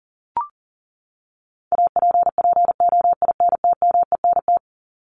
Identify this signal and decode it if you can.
Morse code, heard as a keyed tone, about 40 wpm, 700 Hz. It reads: APPOINTMENT
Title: Top 1751-2000 Words 40wpm